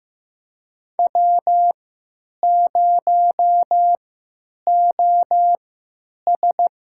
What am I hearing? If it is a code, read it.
Morse code, heard as a keyed tone, about 15 wpm, spaced as for 9 wpm, 700 Hz. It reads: W0OS
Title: USA Call Signs - 3x Character Spacing - 15wpm